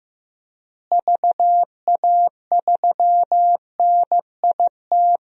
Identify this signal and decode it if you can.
Morse code, heard as a keyed tone, about 15 wpm, 700 Hz. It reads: VA3NIT
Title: DXCC-100 Call Signs - 6 Characters - Repeated 2x - 15wpm